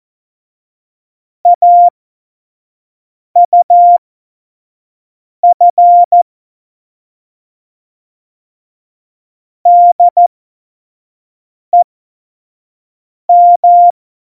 Morse code, heard as a keyed tone, about 14 wpm, spaced as for 5 wpm, 700 Hz. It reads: AUF DEM
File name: Morsekurs / Witze / Witz 18 Juli 17, Tempo 25 F70